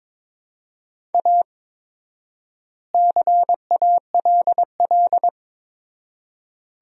Morse code, heard as a keyed tone, about 22 wpm, 700 Hz. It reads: A CALL
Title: 4x Word-Spacing - Sentences from Top 300 Words - 22wpm